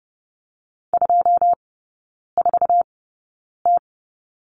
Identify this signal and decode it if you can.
Morse code, heard as a keyed tone, about 30 wpm, spaced as for 9 wpm, 690 Hz. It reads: 24T
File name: Lesson 16e - Letter c - Three Characters 30wpm